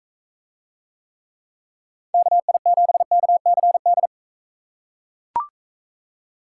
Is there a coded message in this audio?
Yes: Morse code from 2.1 to 4.1 s, heard as a keyed tone, about 41 wpm, 700 Hz. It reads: KI7KCD